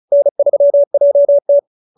Morse code, heard as a keyed tone, about 35 wpm, 590 Hz. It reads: N3JT